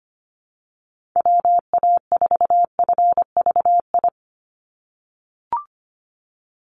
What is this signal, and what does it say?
Morse code, heard as a keyed tone, about 25 wpm, 700 Hz. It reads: WA4FVI
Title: USA Call Signs 25wpm